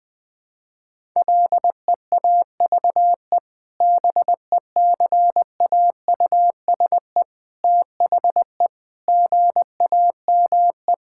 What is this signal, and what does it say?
Morse code, heard as a keyed tone, about 20 wpm, 700 Hz. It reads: LEAVE BECAUSE THE GAME
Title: Sentences with 2-Word Phrases - Spoken Beginning - 20wpm